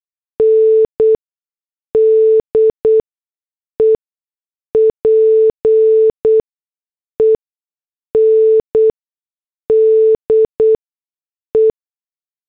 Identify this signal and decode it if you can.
Morse code, heard as a keyed tone, about 8 wpm, 440 Hz. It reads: N D E P E N D E
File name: morse / headlines